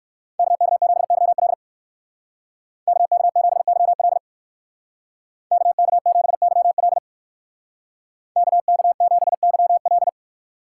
Morse code, heard as a keyed tone, about 56 wpm, 690 Hz. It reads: KK7YL KK7YL KK7YL KK7YL
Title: Speed Racing - USA 2x2 Call Signs 45wpm